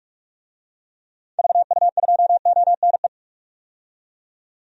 Morse code, heard as a keyed tone, about 42 wpm, 700 Hz. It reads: VU2ONE
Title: DXCC-100 Call Signs - 6 Characters - 45wpm